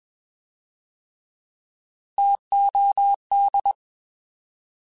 Morse code, heard as a keyed tone, about 21 wpm, 800 Hz. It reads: TOD
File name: Alphabetical / Alphabetical mp3s / Ts @20